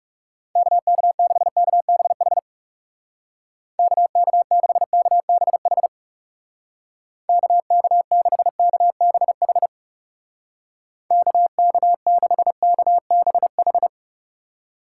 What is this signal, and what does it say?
Morse code, heard as a keyed tone, about 39 wpm, 700 Hz. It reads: KK6KBH KK6KBH KK6KBH KK6KBH